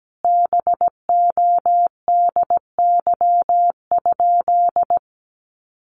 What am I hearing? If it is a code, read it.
Morse code, heard as a keyed tone, about 17 wpm, 700 Hz. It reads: BODY?
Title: Sentences from Top 400 Words 17wpm